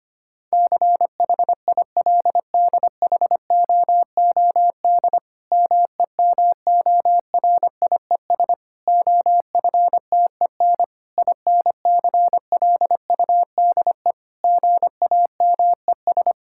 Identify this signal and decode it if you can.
Morse code, heard as a keyed tone, about 25 wpm, 700 Hz. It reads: CHILDHOOD MEMORIES OFTEN INCLUDE GAMES
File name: practice / 25wpm / Difficult Sentences with Hints - 25wpm